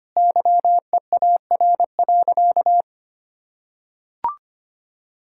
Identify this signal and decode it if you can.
Morse code, heard as a keyed tone, about 25 wpm, 700 Hz. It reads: YEAR.
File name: practice / 25wpm / Sentences with 2-Word Phrases - 25wpm